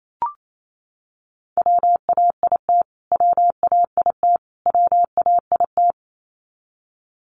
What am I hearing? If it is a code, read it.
Morse code, heard as a keyed tone, about 28 wpm, 700 Hz. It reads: WAIT WAIT WAIT
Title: Top 500 Words Repeated 28wpm